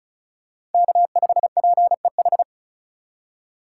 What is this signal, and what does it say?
Morse code, heard as a keyed tone, about 36 wpm, 700 Hz. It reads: K5PEH
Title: USA Call Signs - 2x Character Spacing - 35wpm